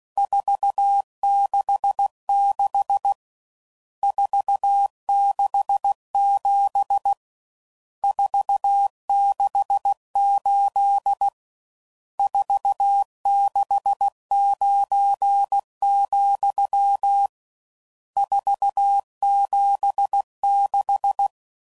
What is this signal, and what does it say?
Morse code, heard as a keyed tone, about 16 wpm, 800 Hz. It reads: 466 467 468 469, 476
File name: CW Code Groups @15WPM / RSTs / Various RST Nos